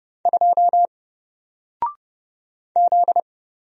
Morse code, heard as a keyed tone, about 30 wpm, 700 Hz. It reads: 2 Z
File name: Single-Letter-Number-30wpm